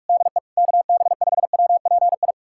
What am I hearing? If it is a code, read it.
Morse code, heard as a keyed tone, about 42 wpm, 700 Hz. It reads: DE KB5WPI
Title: USA Callsign DE Callsign - 6 Characters - 45wpm